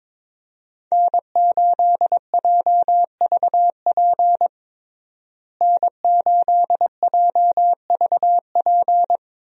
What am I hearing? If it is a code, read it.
Morse code, heard as a keyed tone, about 21 wpm, 700 Hz. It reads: N8JVP N8JVP
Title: USA Call Signs - 5 Characters - Repeated 2x - 22wpm